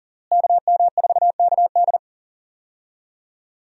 Morse code, heard as a keyed tone, about 37 wpm, 700 Hz. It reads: KM4KD
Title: USA Call Signs - 5 Characters - Repeated 3x - 40wpm